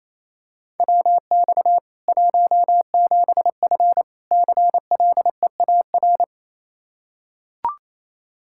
Morse code, heard as a keyed tone, about 28 wpm, 700 Hz. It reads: WX 17F CLEAR
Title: Practice QSO - Exchange 2 - Info - 28wpm